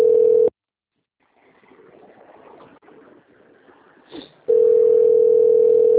Ringback tone (North America), 440 + 480 Hz.